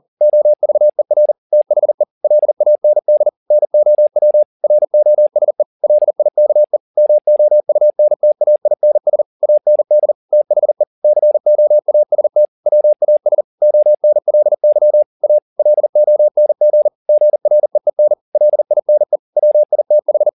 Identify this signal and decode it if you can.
Morse code, heard as a keyed tone, about 40 wpm, 600 Hz. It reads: OVER THE LAND NOW ROSE LIKE MOUNTAINS AND THE COAST WAS ONLY A LONG GREEN LINE WITH